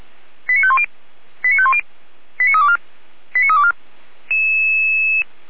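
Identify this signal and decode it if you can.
Five-tone selective call (ZVEI), heard as quick runs of five tones, tones of about 70 ms. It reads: 89510 89510 89324 89324
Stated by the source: ZVEI five-tone selective call 89305 received by SDR, logged by Feuerwehr Orlamünde fire brigade.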